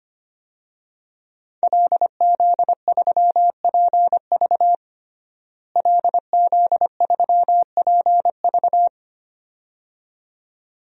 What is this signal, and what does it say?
Morse code, heard as a keyed tone, about 25 wpm, 700 Hz. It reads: LZ3PV LZ3PV
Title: DXCC-100 Call Signs - 5 Characters - Repeated 2x - 25wpm